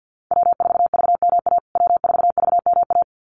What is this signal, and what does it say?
Morse code, heard as a keyed tone, about 48 wpm, 700 Hz. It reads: W4VMA W4VMA
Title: USA Call Signs Repeated 2x 50wpm